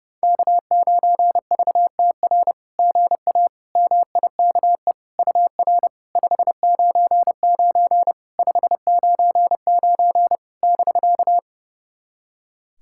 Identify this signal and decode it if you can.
Morse code, heard as a keyed tone, about 30 wpm, 700 Hz. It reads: K9VTR GA MIKE UR 599 599 <BK>
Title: SOTA QSO Practice 30wpm